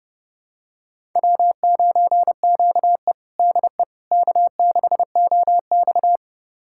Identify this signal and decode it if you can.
Morse code, heard as a keyed tone, about 30 wpm, 700 Hz. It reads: W9QE DE K6OX
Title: USA Callsign DE Callsign - 4 Characters - 30wpm